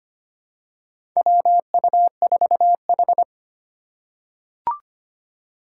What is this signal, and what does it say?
Morse code, heard as a keyed tone, about 25 wpm, 700 Hz. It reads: WU4H
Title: USA Call Signs - 4 Characters - 6x Character Spacing - 25wpm